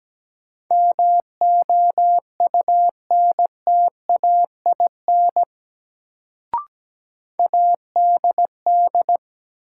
Morse code, heard as a keyed tone, about 17 wpm, 700 Hz. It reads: MOUNTAIN ADD
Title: Sentences from Top 1000 Words - Review Part 1 - New Word Drill - 17wpm